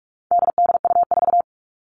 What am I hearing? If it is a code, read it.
Morse code, heard as a keyed tone, about 44 wpm, 700 Hz. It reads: DDU4
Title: All Cut Numbers - 4-Digit 45wpm